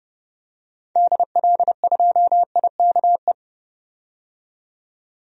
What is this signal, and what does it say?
Morse code, heard as a keyed tone, about 30 wpm, 700 Hz. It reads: DL2IKE
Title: DXCC-100 Call Signs - 6 Characters - 30wpm